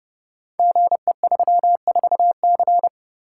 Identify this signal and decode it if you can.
Morse code, heard as a keyed tone, about 30 wpm, 700 Hz. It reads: GE34C
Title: Code Groups - 5 Letter:Numbers - 30wpm